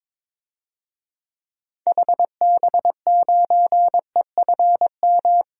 Morse code, heard as a keyed tone, about 22 wpm, 700 Hz. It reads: HB9EFM
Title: DXCC-100 Call Signs - 6 Characters - Repeated 2x - 22wpm